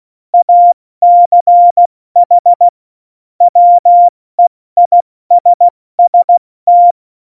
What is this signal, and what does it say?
Morse code, heard as a keyed tone, about 16 wpm, 700 Hz. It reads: ACH WEISST
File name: Morsekurs / Witze / Witz 22. Mai 2018, 70 F80